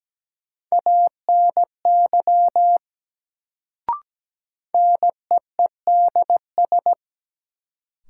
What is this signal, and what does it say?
Morse code, heard as a keyed tone, about 17 wpm, 700 Hz. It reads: ANY NEEDS
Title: Sets of 3 Words - New Word Drill - 17wpm